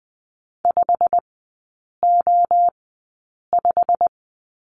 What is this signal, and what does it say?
Morse code, heard as a keyed tone, about 20 wpm, spaced as for 8 wpm, 700 Hz. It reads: 5O5
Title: 087 Lesson 15e - Number 5 - Three Characters 20wpm